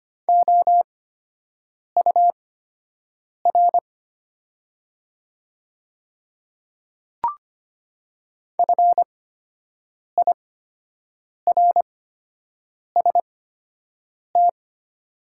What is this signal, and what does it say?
Morse code, heard as a keyed tone, about 25 wpm, spaced as for 7 wpm, 700 Hz. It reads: OUR FIRST
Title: 8x Character Spacing - Top 300 Words - 25wpm